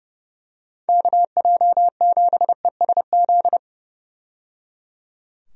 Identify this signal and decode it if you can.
Morse code, heard as a keyed tone, about 30 wpm, 700 Hz. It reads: KJ7ESZ
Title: USA Call Signs - 6 Characters - 30wpm